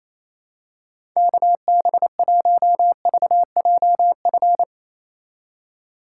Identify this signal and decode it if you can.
Morse code, heard as a keyed tone, about 28 wpm, 700 Hz. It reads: KB1VJF
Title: USA Call Signs - 6 Characters - 4x Character Spacing - 28wpm